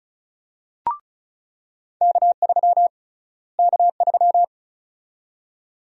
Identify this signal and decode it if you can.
Morse code, heard as a keyed tone, about 36 wpm, 700 Hz. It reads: K3 K3
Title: USA Call Sign Prefixes - Repeated 2x - 35wpm